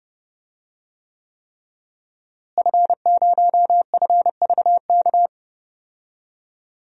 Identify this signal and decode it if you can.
Morse code, heard as a keyed tone, about 30 wpm, 700 Hz. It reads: F0FVK
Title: DXCC-100 Call Signs - 5 Characters - Repeated 3x - 30wpm